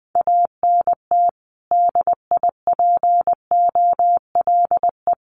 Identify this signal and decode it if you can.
Morse code, heard as a keyed tone, about 20 wpm, 700 Hz. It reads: ANT DIPOLE
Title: Common QSO Phrases 20wpm